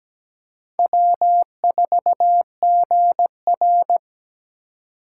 Morse code, heard as a keyed tone, about 20 wpm, 700 Hz. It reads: W4GR
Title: Speed Racing - USA 1x2 Call Signs 17wpm